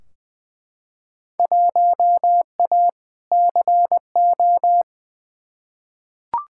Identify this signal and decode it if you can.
Morse code, heard as a keyed tone, about 20 wpm, 700 Hz. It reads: 1A CO
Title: Field Day Exchanges - 20wpm